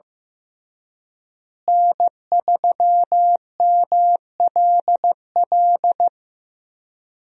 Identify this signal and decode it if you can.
Morse code, heard as a keyed tone, about 15 wpm, 700 Hz. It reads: N3MLL
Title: USA Call Signs - 5 Characters - Repeated 3x - 15wpm